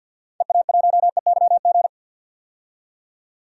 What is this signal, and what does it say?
Morse code, heard as a keyed tone, about 46 wpm, 700 Hz. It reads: EA1EYG
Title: DXCC-100 Call Signs - 6 Characters - 50wpm